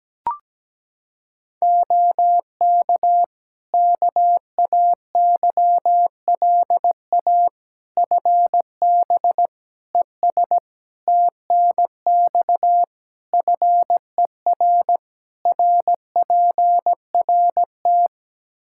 Morse code, heard as a keyed tone, about 17 wpm, 700 Hz. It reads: OK KAYLA FB ES TNX FER RPRT
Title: K1USN SST QSO Practice 17wpm